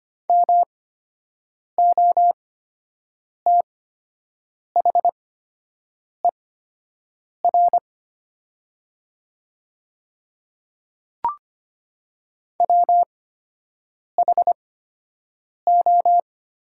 Morse code, heard as a keyed tone, about 25 wpm, spaced as for 7 wpm, 700 Hz. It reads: MOTHER WHO
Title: 8x Character Spacing - Top 500 Words - 25wpm